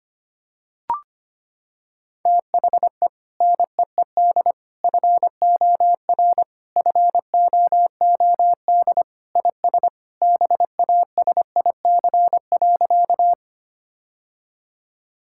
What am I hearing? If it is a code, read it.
Morse code, heard as a keyed tone, about 25 wpm, 700 Hz. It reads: THE NEED FOR FOOD IS BASIC.
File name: practice / 25wpm / Sentences with 3-Word Phrases - 25wpm